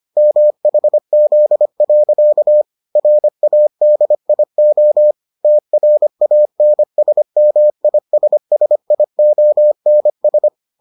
Morse code, heard as a keyed tone, about 25 wpm, 600 Hz. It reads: MHZ. RADIO TRANSMISSIONS